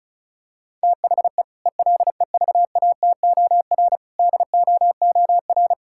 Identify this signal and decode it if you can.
Morse code, heard as a keyed tone, about 36 wpm, 700 Hz. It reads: THE ELEVATOR DOOR
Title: Sentences from Top 1600 Words 35wpm